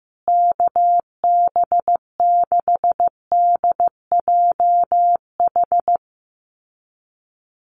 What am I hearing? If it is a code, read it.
Morse code, heard as a keyed tone, about 15 wpm, 700 Hz. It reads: KB6DJH
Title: USA Call Signs Repeated 3x 15wpm